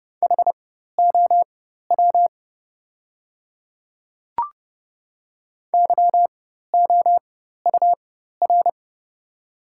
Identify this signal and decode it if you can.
Morse code, heard as a keyed tone, about 30 wpm, spaced as for 14 wpm, 700 Hz. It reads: HOW YOUR
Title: 4x Character Spacing - Top 100 Words - 30wpm